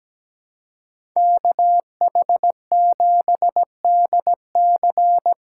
Morse code, heard as a keyed tone, about 17 wpm, 700 Hz. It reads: KH7DC
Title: USA Callsign DE Callsign - 5 Characters - 17wpm